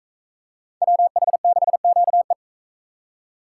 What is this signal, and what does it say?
Morse code, heard as a keyed tone, about 39 wpm, 700 Hz. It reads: WH6QE